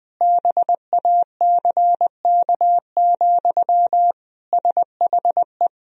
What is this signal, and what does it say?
Morse code, heard as a keyed tone, about 20 wpm, 700 Hz. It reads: BACK, SHE
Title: Sentences from Top 1000 Words - Review Part 2 - 20wpm